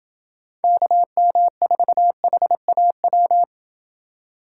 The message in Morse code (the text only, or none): KM4HAW